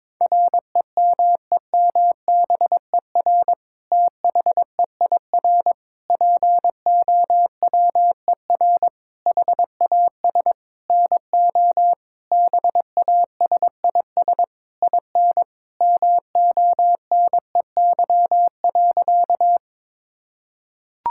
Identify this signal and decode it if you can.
Morse code, heard as a keyed tone, about 22 wpm, 700 Hz. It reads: REMEMBER THEIR POWER HAS NO BASIS IN MONEY.
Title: Sentences from Top 1000 Words - Review Part 5 - 22wpm